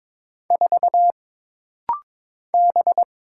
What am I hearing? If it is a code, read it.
Morse code, heard as a keyed tone, about 22 wpm, 700 Hz. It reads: 4 B
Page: This rapid-fire timing is geared to push your ICR (Instant Character Recognition) skills.